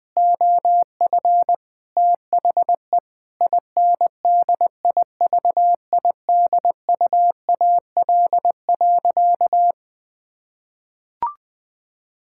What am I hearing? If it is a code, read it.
Morse code, heard as a keyed tone, about 20 wpm, 700 Hz. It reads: OF THE INDIVIDUAL.
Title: Sentences from Top 1000 Words - Review Part 6 - 20wpm